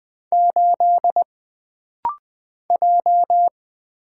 Morse code, heard as a keyed tone, about 20 wpm, 700 Hz. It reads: 8 J
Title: Single Letter-Number - Rapid-Fire 20wpm